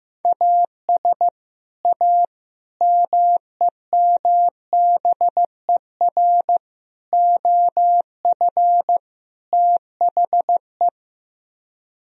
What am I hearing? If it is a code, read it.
Morse code, heard as a keyed tone, about 16 wpm, 700 Hz. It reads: AS A MEMBER OF THE
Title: Speed-Racing - Sets of 5 Words 15wpm